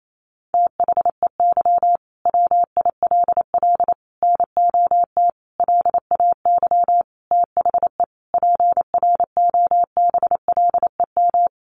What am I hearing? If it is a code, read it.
Morse code, heard as a keyed tone, about 28 wpm, 700 Hz. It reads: THEY WILL NOT LAY THE PROBLEM
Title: Sentences from Top 1500 Words 28wpm